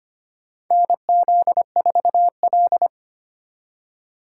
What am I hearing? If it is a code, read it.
Morse code, heard as a keyed tone, about 25 wpm, 700 Hz. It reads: NZ4L
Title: USA Call Signs - 4 Characters - 4x Character Spacing - 25wpm